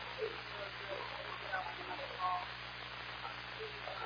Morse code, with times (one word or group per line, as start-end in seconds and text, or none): none